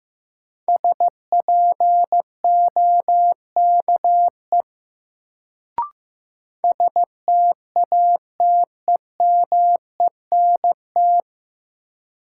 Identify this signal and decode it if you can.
Morse code, heard as a keyed tone, about 15 wpm, 700 Hz. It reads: SPOKE STATEMENT
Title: Sentences from Top 1200 Words - New Word Drill - 15wpm